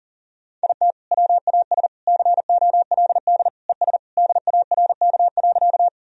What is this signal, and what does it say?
Morse code, heard as a keyed tone, about 40 wpm, 700 Hz. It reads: IT WAS COLD ES DARK.
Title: Sentences from Top 500 Words and Abbreviations 40wpm